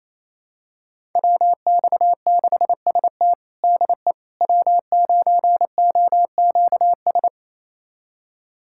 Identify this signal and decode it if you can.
Morse code, heard as a keyed tone, about 28 wpm, 700 Hz. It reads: WX6ST DE W9OQS
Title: USA Callsign DE Callsign - 5 Characters - 28wpm